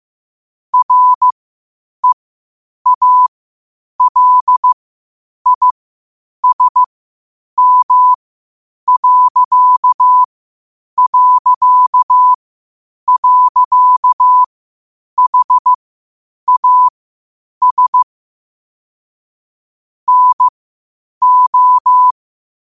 Morse code, heard as a keyed tone, about 15 wpm, spaced as for 8 wpm, 1000 Hz. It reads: REALISM...HAS NO
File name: qotdfiles / 8wpm